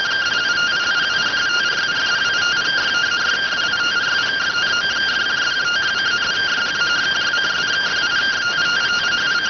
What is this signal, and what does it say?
Two-tone teleprinter signal, about 170 Hz shift.